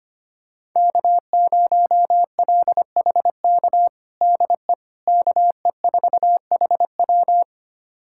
Morse code, heard as a keyed tone, about 25 wpm, 700 Hz. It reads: K0LHK DE KE4HW
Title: USA Callsign DE Callsign - 5 Characters - 25wpm